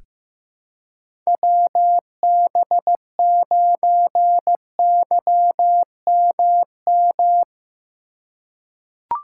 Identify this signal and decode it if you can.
Morse code, heard as a keyed tone, about 15 wpm, 700 Hz. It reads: WB9YMM
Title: USA Call Signs - 6 Characters - 6x Character Spacing - 15wpm